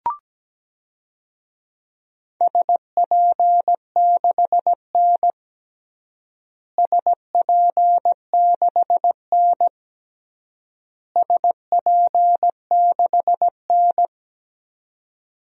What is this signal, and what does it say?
Morse code, heard as a keyed tone, about 17 wpm, 700 Hz. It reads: SP6N SP6N SP6N
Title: DXCC-100 Call Signs - 4 Characters - Repeated 3x - 17wpm